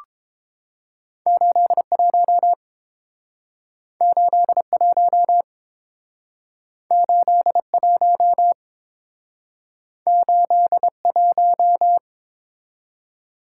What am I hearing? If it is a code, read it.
Morse code, heard as a keyed tone, about 30 wpm, 700 Hz. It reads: 81 81 81 81